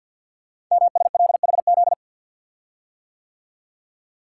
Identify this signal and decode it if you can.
Morse code, heard as a keyed tone, about 47 wpm, 700 Hz. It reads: MSLHZ
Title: Code Groups - 5 Letter:Numbers - 50wpm